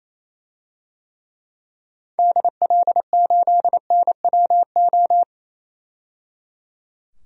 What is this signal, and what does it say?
Morse code, heard as a keyed tone, about 28 wpm, 700 Hz. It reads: DL8NWO
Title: DXCC-100 Call Signs - 6 Characters - 28wpm